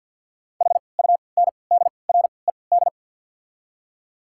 Morse code, heard as a keyed tone, about 52 wpm, spaced as for 28 wpm, 700 Hz. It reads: HUNDRED
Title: 3x Character Spacing - Top 500 Words - 50wpm